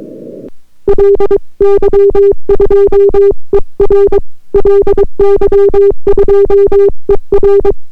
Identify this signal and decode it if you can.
Morse code, heard as a keyed tone, about 22 wpm, 400 Hz. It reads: LY2ER LY2ER